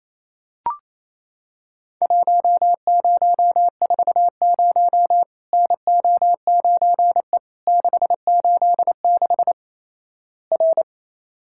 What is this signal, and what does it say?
Morse code, heard as a keyed tone, about 28 wpm, 700 Hz. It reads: 1040 NO9E 686 R
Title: Worked All Europe DX Contest - QTC Exchange - 28wpm